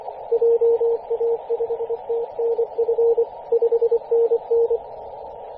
Morse code, heard as a keyed tone, about 24 wpm, 460 Hz, with radio noise. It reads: JA5TNF 5NN
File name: t32fs-10mhz-cw